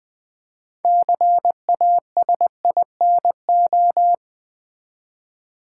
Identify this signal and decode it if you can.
Morse code, heard as a keyed tone, about 20 wpm, 700 Hz. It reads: CASINO
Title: Build A Word - 2 Letters At A Time - 6-Letter Words - 20wpm